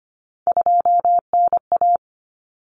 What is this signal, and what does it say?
Morse code, heard as a keyed tone, about 25 wpm, 700 Hz. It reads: 2NA
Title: Common Cut Numbers - 3-Digit 25wpm